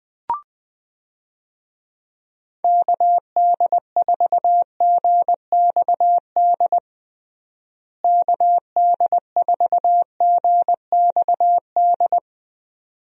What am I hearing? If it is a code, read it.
Morse code, heard as a keyed tone, about 21 wpm, 700 Hz. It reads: KD4GXD KD4GXD
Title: USA Call Signs - 6 Characters - Repeated 2x - 20wpm